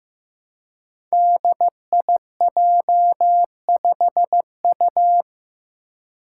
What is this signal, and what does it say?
Morse code, heard as a keyed tone, about 15 wpm, 700 Hz. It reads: DIJ5U
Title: Five Challenging Letter-Number Combinations 15wpm